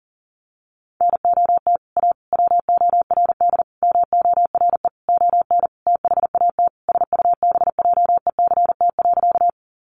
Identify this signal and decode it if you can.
Morse code, heard as a keyed tone, about 37 wpm, 700 Hz. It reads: NOT A WORD MORE ON THAT SUBJECT.
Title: Sentences from Top 800 Words 40wpm